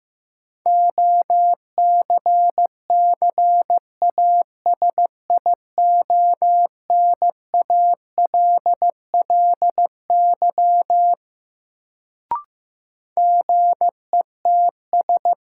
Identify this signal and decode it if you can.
Morse code, heard as a keyed tone, about 15 wpm, 700 Hz. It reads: OCCASIONALLY GETS
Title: Sentences from Top 1300 Words - New Word Drill Alt - 15wpm